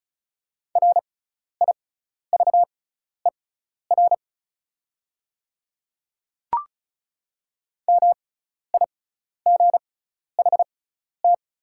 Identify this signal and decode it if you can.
Morse code, heard as a keyed tone, about 36 wpm, spaced as for 12 wpm, 700 Hz. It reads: RIVER MIGHT
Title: 6x Character Spacing - Top 300 Words - 35wpm